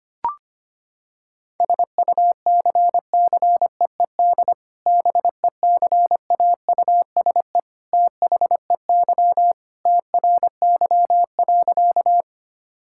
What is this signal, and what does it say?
Morse code, heard as a keyed tone, about 25 wpm, 700 Hz. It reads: SUCCEED BECAUSE THEY TRY.
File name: practice / 25wpm / Sentences with 2-Word Phrases - 25wpm